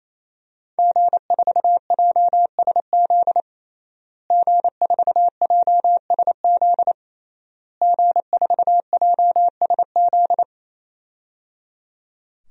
Morse code, heard as a keyed tone, about 28 wpm, 700 Hz. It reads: G4JSZ G4JSZ G4JSZ
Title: DXCC-100 Call Signs - 5 Characters - Repeated 3x - 28wpm